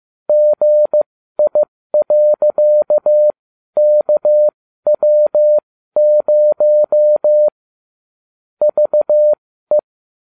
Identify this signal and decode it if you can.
Morse code, heard as a keyed tone, about 15 wpm, spaced as for 12 wpm, 600 Hz. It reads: GI.KW0 VE